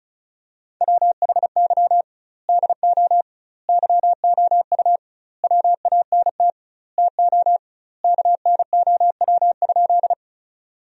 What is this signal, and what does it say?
Morse code, heard as a keyed tone, about 36 wpm, 700 Hz. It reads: WHY DO YOU WANT TO KNOW?
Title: 2x Word-Spacing - Sentences from Top 200 Words - 35wpm